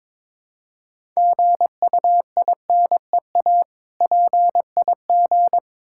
Morse code, heard as a keyed tone, about 22 wpm, 700 Hz. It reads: GUINEA PIG
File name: Difficult Words with Hints - 22wpm